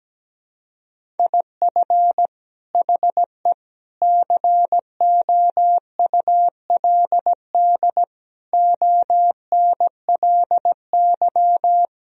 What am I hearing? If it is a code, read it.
Morse code, heard as a keyed tone, about 17 wpm, 700 Hz. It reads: IF HE COULD ONLY
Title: Sentences from Top 1000 Words - Review Part 1 - 17wpm